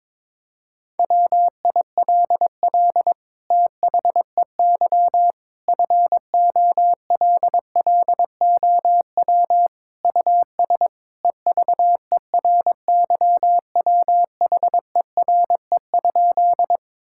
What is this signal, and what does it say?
Morse code, heard as a keyed tone, about 22 wpm, 700 Hz. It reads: WILL THEY FOLLOW US EVERYWHERE?
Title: Sentences from Top 1600 Words 22wpm